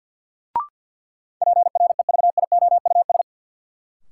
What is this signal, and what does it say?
Morse code, heard as a keyed tone, about 52 wpm, 700 Hz. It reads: PREVIOUS
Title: Sentences from Top 1500 Words - New Word Drill - 50wpm